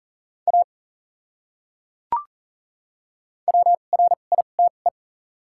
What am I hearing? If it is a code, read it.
Morse code, heard as a keyed tone, about 40 wpm, spaced as for 29 wpm, 700 Hz. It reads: A WRITE
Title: 2x Character Spacing - Top 100 Words - 40wpm